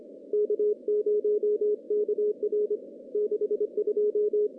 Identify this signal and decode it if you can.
Morse code, heard as a keyed tone, about 27 wpm, 420 Hz. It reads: K0KR 62
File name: contests / CW-SS-k5tr- -1500